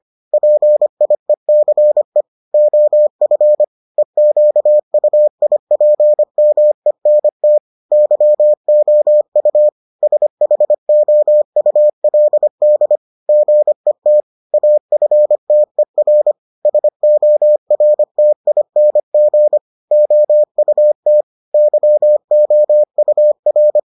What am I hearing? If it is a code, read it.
Morse code, heard as a keyed tone, about 25 wpm, 600 Hz. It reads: PIECE OF EQUIPMENT YOU SHOULD GET AFTER SORTING OUT YOUR